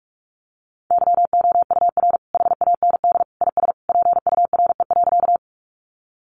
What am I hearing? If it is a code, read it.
Morse code, heard as a keyed tone, about 44 wpm, 700 Hz. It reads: YOUR HAND IS PURE.
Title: Sentences from Top 1500 Words 45wpm